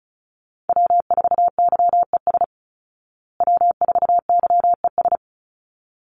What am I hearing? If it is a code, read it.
Morse code, heard as a keyed tone, about 34 wpm, 700 Hz. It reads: W4YES W4YES
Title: Call Signs Repeated 3x - 35wpm